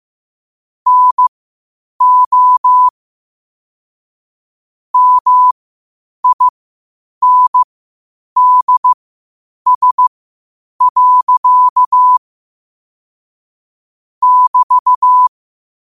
Morse code, heard as a keyed tone, about 15 wpm, spaced as for 8 wpm, 1000 Hz. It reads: NO MINDS. <BT>
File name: qotdfiles / 8wpm